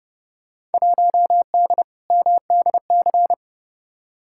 Morse code, heard as a keyed tone, about 30 wpm, 700 Hz. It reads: 1D MDC
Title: Field Day Exchanges - 30wpm